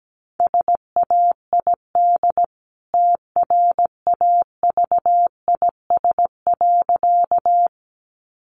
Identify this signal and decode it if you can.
Morse code, heard as a keyed tone, about 17 wpm, 700 Hz. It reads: SAID TRAVIS.
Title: A Sound Of Thunder 17wpm